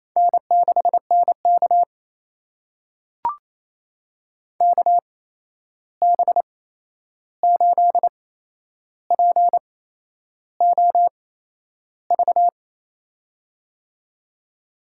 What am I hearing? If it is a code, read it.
Morse code, heard as a keyed tone, about 28 wpm, spaced as for 11 wpm, 700 Hz. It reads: N6NK KB8POV
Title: USA Call Signs - 8x Character Spacing - 28wpm